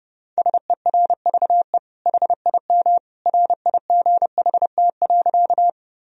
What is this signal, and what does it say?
Morse code, heard as a keyed tone, about 32 wpm, 700 Hz. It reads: SERVE HIM RIGHT.
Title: Sentences from Top 1000 Words - Review Part 2 - 30wpm